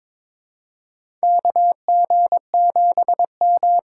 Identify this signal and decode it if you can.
Morse code, heard as a keyed tone, about 22 wpm, 700 Hz. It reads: KG7M
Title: USA Call Signs - 6 Characters - 6x Character Spacing - 22wpm